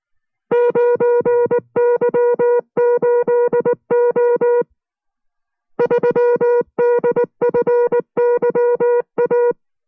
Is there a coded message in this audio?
Yes: Morse code from 0.5 to 9.5 s, heard as a keyed tone, about 19 wpm, 480 Hz. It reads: 9Y8O 3DFYA